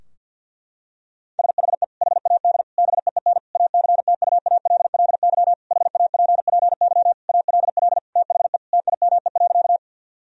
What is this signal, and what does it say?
Morse code, heard as a keyed tone, about 50 wpm, 700 Hz. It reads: SHE HAD BEEN ACTUALLY HAPPY ALL THE TIME.
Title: Sentences from Top 1000 Words - Review Part 4 - 50wpm